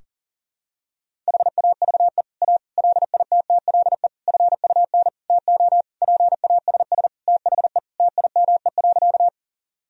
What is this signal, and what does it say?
Morse code, heard as a keyed tone, about 40 wpm, 700 Hz. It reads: HAVE A LITTLE FUN TO PASS THE TIME.
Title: Sentences from Top 1000 Words - Review Part 4 - 40wpm